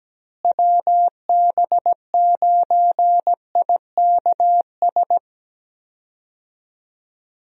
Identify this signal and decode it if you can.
Morse code, heard as a keyed tone, about 17 wpm, 700 Hz. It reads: WB9IKS
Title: USA Call Signs - 6 Characters - Repeated 2x - 17wpm